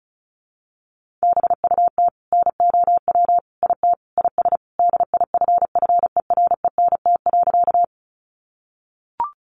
Morse code, heard as a keyed tone, about 34 wpm, 700 Hz. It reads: BUT NOW IT IS DIFFERENT.
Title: Sentences from Top 600 Words 35wpm